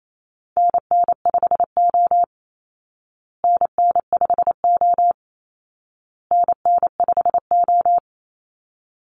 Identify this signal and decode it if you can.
Morse code, heard as a keyed tone, about 28 wpm, 700 Hz. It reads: NN5O NN5O NN5O
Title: Call Signs Repeated 2x - 28wpm